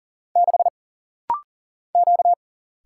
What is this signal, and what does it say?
Morse code, heard as a keyed tone, about 39 wpm, 700 Hz. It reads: 6 Q
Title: Single Letter-Number - Rapid-Fire 40wpm